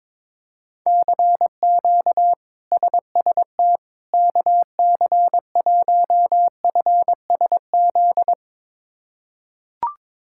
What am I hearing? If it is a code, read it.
Morse code, heard as a keyed tone, about 22 wpm, 700 Hz. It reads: CQ SST KC1FSZ
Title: K1USN SST QSO Practice - 22wpm